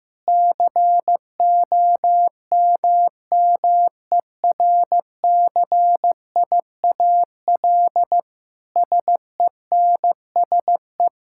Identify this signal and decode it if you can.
Morse code, heard as a keyed tone, about 15 wpm, 700 Hz. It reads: COMMERCIAL SENSE
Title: Sentences from Top 1000 Words - Review Part 5 - 15wpm